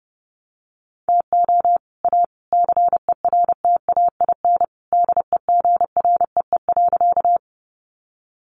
Morse code, heard as a keyed tone, about 30 wpm, 700 Hz. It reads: TO A CERTAIN DEGREE.
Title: Sentences from Top 1500 Words 30wpm